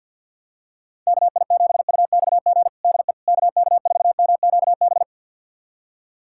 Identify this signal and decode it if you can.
Morse code, heard as a keyed tone, about 51 wpm, 700 Hz. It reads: KI7UXG DE KK4MQB
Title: USA Callsign DE Callsign - 6 Characters - 50wpm